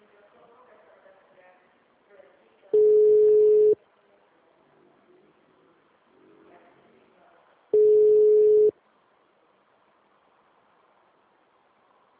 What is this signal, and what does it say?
Ringback tone (Europe, 425 Hz), with speech.